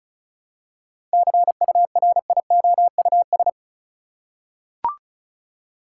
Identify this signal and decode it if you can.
Morse code, heard as a keyed tone, about 34 wpm, 700 Hz. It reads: CURIOUS
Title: Seven Letter Words - 35wpm